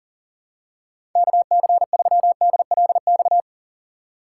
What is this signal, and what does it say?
Morse code, heard as a keyed tone, about 37 wpm, 700 Hz. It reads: KC3DLX
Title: USA Call Signs - 6 Characters - 8x Character Spacing - 40wpm